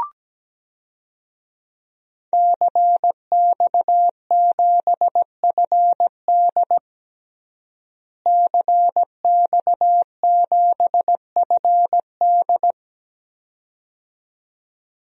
Morse code, heard as a keyed tone, about 17 wpm, 700 Hz. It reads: CX7FD CX7FD
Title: DXCC-100 Call Signs - 5 Characters - Repeated 2x - 17wpm